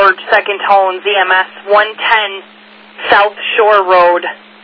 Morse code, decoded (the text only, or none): none